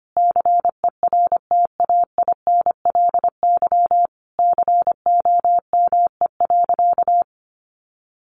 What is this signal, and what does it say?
Morse code, heard as a keyed tone, about 25 wpm, 700 Hz. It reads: CERTAINLY COME.
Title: Sentences from Top 900 Words 25wpm